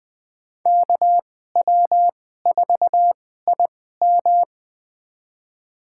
Morse code, heard as a keyed tone, about 20 wpm, spaced as for 14 wpm, 700 Hz. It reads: KW4IM
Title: USA Call Signs - 5 Characters - 2x Character Spacing - 20wpm